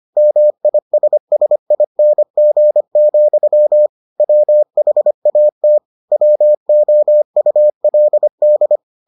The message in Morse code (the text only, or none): MISSING, WHAT WOULD